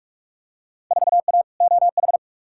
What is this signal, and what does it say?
Morse code, heard as a keyed tone, about 42 wpm, 700 Hz. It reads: 4A OH
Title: Field Day Exchanges - 45wpm